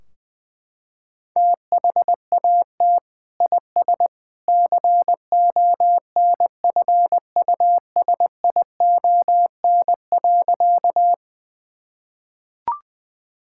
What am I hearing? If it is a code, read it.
Morse code, heard as a keyed tone, about 20 wpm, 700 Hz. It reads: THAT IS CONFUSION.
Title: Sentences from Top 1600 Words 20wpm